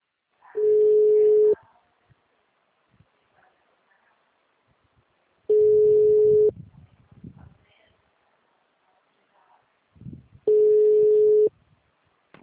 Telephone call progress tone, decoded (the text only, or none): ringback tone